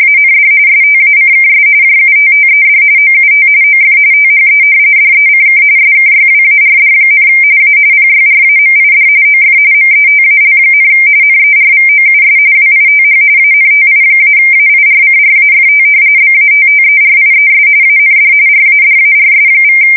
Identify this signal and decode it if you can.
RTTY, about 45.45 baud. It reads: AND STACKS JUTTED OUT OF THE DECK SKYWARD, A PYRAMID OF METAL, NERVOUS WITH ANTI-AIRCRAFT BATTERIES AND RADARS=
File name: fsk170 / battleshipnj